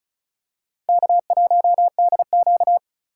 Morse code, heard as a keyed tone, about 36 wpm, 700 Hz. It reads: K1DQ
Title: USA Call Signs - 2x Character Spacing - 35wpm